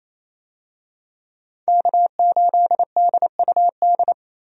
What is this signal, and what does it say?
Morse code, heard as a keyed tone, about 28 wpm, 700 Hz. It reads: K8DUD
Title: USA Call Signs - 5 Characters - 28wpm